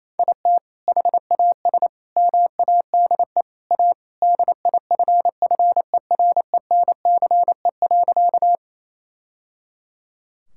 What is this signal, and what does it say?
Morse code, heard as a keyed tone, about 27 wpm, 700 Hz. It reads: IT HAS MADE A DIFFERENCE.
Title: Sentences from Top 1000 Words - Review Part 5 - 28wpm